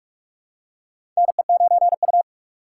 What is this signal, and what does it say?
Morse code, heard as a keyed tone, about 44 wpm, 700 Hz. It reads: NE9U
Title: K1USN SST QSO Practice - 45wpm